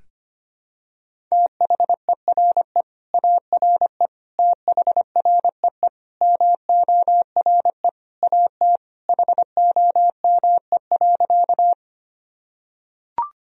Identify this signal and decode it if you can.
Morse code, heard as a keyed tone, about 25 wpm, 700 Hz. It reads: THERE ARE THREE MORE AT HOME.
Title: Sentences from Top 1000 Words - Review Part 1 - 25wpm